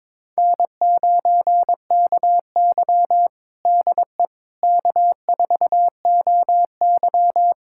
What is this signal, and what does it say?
Morse code, heard as a keyed tone, about 21 wpm, 700 Hz. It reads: N9KY DE K4OY
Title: USA Callsign DE Callsign - 4 Characters - 22wpm